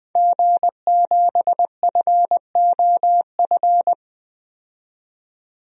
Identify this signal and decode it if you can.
Morse code, heard as a keyed tone, about 20 wpm, 700 Hz. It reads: G7FOF
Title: DXCC-100 Call Signs - 5 Characters - 20wpm